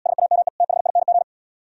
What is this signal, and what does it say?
Morse code, heard as a keyed tone, about 76 wpm, 700 Hz. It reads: SAME THING